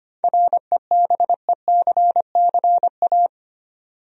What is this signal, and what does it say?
Morse code, heard as a keyed tone, about 25 wpm, 700 Hz. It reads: REBECCA
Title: USA Common Names - 25wpm